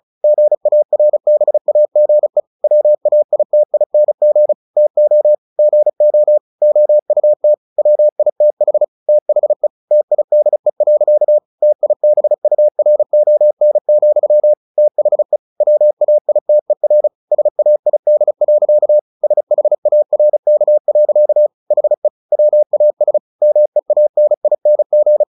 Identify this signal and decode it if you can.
Morse code, heard as a keyed tone, about 34 wpm, 600 Hz. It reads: GARBAGE WAITING TO GO OUT WITH THE TIDE. TIBURON, THE WAITER SAID. SHARK. HE WAS MEANING
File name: the old man and the sea 2 35wpm